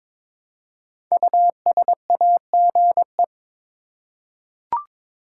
Morse code, heard as a keyed tone, about 22 wpm, 700 Hz. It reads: USAGE
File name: 5 Letter Words - First 3 Letters Given - 22wpm